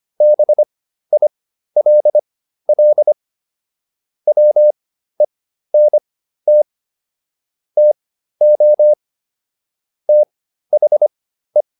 Morse code, heard as a keyed tone, about 25 wpm, spaced as for 13 wpm, 600 Hz. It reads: BILL WENT TO THE
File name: SS109 13